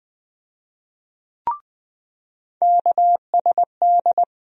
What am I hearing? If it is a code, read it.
Morse code, heard as a keyed tone, about 20 wpm, 700 Hz. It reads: KSD
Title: USA Call Sign Postfixes - 3 Letters 20wpm